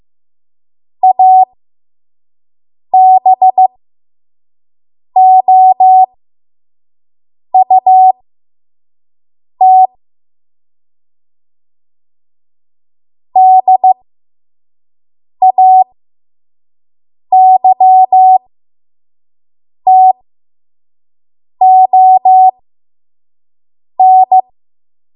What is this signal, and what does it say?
Morse code, heard as a keyed tone, about 15 wpm, spaced as for 5 wpm, 750 Hz. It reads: ABOUT DAYTON
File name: Morse / 05WPM